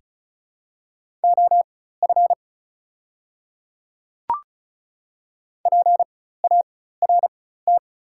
Morse code, heard as a keyed tone, about 35 wpm, spaced as for 16 wpm, 700 Hz. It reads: OF PART
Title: 4x Character Spacing - Top 300 Words - 35wpm